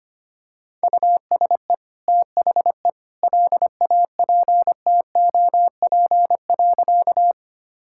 Morse code, heard as a keyed tone, about 25 wpm, 700 Hz. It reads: USE THE LAPTOP.
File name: Easy Sentences with Hints - 25wpm